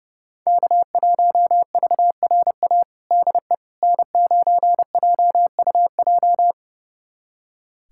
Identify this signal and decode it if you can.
Morse code, heard as a keyed tone, about 32 wpm, 700 Hz. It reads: K1VRA DE N9JUJ
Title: USA Callsign DE Callsign - 5 Characters - 30wpm